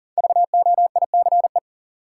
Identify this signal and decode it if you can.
Morse code, heard as a keyed tone, about 40 wpm, 700 Hz. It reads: VOICE
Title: Sentences from Top 500 Words - New Word Drill - 40wpm